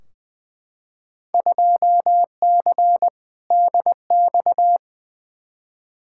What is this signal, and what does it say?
Morse code, heard as a keyed tone, about 20 wpm, 700 Hz. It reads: 2C DX
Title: Field Day Exchanges Encore - 20wpm